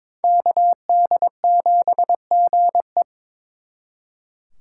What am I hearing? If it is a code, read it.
Morse code, heard as a keyed tone, about 26 wpm, 700 Hz. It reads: KD7GE
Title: Speed Racing - USA Call Signs 22wpm